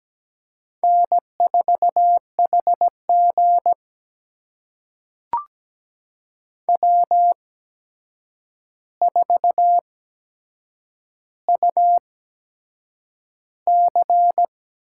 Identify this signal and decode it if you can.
Morse code, heard as a keyed tone, about 17 wpm, spaced as for 7 wpm, 700 Hz. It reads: N4HG W4UC
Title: USA Call Signs - 4 Characters - 8x Character Spacing - 17wpm